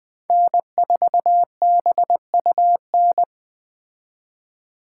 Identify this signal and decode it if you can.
Morse code, heard as a keyed tone, about 20 wpm, 700 Hz. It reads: N4BUN
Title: 2023 POTA Hunter Call Signs - 20wpm